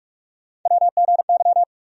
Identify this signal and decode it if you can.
Morse code, heard as a keyed tone, about 46 wpm, 700 Hz. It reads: WGY
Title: USA Call Sign Postfixes - 3 Letters 45wpm